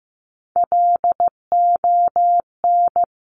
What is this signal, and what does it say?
Morse code, heard as a keyed tone, about 15 wpm, 690 Hz. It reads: LON
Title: Top Three Letter Combinations in Words 15wpm